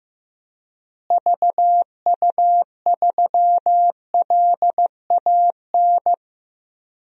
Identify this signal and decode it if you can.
Morse code, heard as a keyed tone, about 15 wpm, 700 Hz. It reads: VU3LAN
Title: DXCC-100 Call Signs - 6 Characters - Repeated 3x - 15wpm